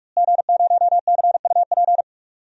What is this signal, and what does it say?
Morse code, heard as a keyed tone, about 43 wpm, 700 Hz. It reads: G0CUP
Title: DXCC-100 Call Signs - 5 Characters - Repeated 3x - 45wpm